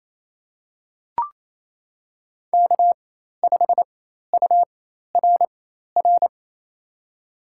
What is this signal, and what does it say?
Morse code, heard as a keyed tone, about 28 wpm, spaced as for 13 wpm, 700 Hz. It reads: K5URR
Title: USA Call Signs - 5 Characters - 4x Character Spacing - 28wpm